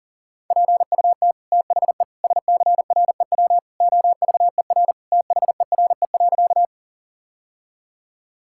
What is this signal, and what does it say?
Morse code, heard as a keyed tone, about 40 wpm, 700 Hz. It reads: PUT THE SCREW OVER THERE.
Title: Sentences from Top 1600 Words 40wpm